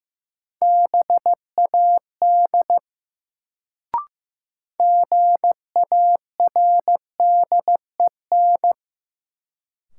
Morse code, heard as a keyed tone, about 15 wpm, 700 Hz. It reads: BAD GARDEN
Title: Sentences from Top 700 Words - New Word Drill - 15wpm